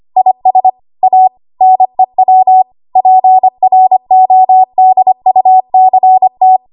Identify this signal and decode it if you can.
Morse code, heard as a keyed tone, about 25 wpm, 750 Hz. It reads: IS A NEW PRODUCT
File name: Morse / 25WPM